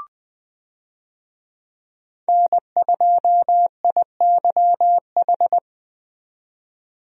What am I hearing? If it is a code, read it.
Morse code, heard as a keyed tone, about 20 wpm, 700 Hz. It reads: N2IYH
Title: USA Call Signs - 5 Characters - 20wpm